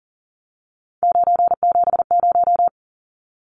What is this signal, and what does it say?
Morse code, heard as a keyed tone, about 40 wpm, 700 Hz. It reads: 970
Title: Numbers - 3-Digit 40wpm